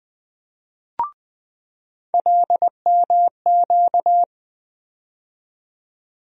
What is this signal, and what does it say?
Morse code, heard as a keyed tone, about 20 wpm, 700 Hz. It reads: LMQ
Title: USA Call Sign Postfixes - 3 Letters 20wpm